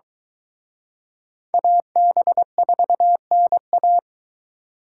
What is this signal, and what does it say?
Morse code, heard as a keyed tone, about 23 wpm, 700 Hz. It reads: AB4NA